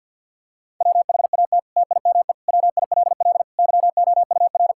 Morse code, heard as a keyed tone, about 50 wpm, 700 Hz. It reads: WHAT TIME WILL YOUR
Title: Sentences from Top 1000 Words - Review Part 5 - 50wpm